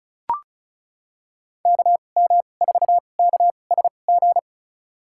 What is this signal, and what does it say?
Morse code, heard as a keyed tone, about 34 wpm, spaced as for 25 wpm, 700 Hz. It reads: KM4KSG
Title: USA Call Signs - 2x Character Spacing - 35wpm